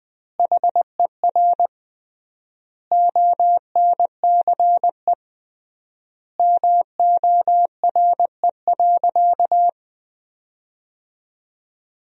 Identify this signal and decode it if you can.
Morse code, heard as a keyed tone, about 20 wpm, 700 Hz. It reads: HER ONCE MORE.
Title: 3x Word-Spacing - Sentences from Top 500 Words - 20wpm